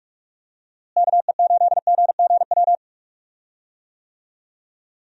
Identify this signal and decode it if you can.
Morse code, heard as a keyed tone, about 46 wpm, 700 Hz. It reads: KE8GGW
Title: USA Call Signs - 6 Characters - Repeated 3x - 45wpm